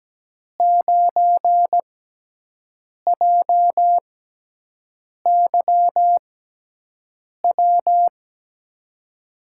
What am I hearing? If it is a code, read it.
Morse code, heard as a keyed tone, about 17 wpm, spaced as for 6 wpm, 700 Hz. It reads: 9JYW
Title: USA Call Signs - 6x Character Spacing - 17wpm